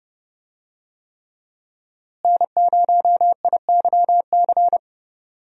Morse code, heard as a keyed tone, about 30 wpm, 700 Hz. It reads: N0IYC
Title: USA Call Signs - 5 Characters - Repeated 3x - 30wpm